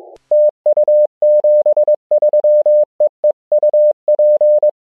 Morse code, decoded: TU 73 EE UP